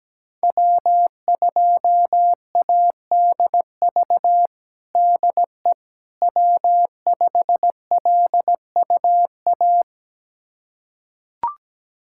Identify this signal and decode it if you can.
Morse code, heard as a keyed tone, about 17 wpm, 700 Hz. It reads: W2ADV DE W5LUA
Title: USA Callsign DE Callsign - 5 Characters - 17wpm